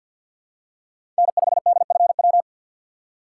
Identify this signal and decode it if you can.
Morse code, heard as a keyed tone, about 52 wpm, 700 Hz. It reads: N5DFW
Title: USA Call Signs - 5 Characters - 6x Character Spacing - 50wpm